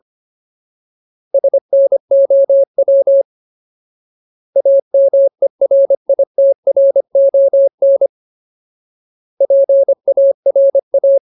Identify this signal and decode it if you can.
Morse code, heard as a keyed tone, about 25 wpm, 550 Hz. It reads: SNOW AMERITRON PARA
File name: Ultimate Word List / 25WPM Ultimate